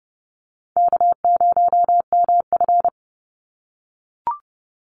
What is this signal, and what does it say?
Morse code, heard as a keyed tone, about 30 wpm, 700 Hz. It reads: K0MF
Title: USA Call Signs 30wpm